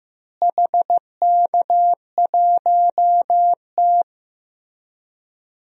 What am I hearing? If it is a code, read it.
Morse code, heard as a keyed tone, about 15 wpm, 700 Hz. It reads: HK1T
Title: DXCC-100 Call Signs - 4 Characters - Repeated 2x - 15wpm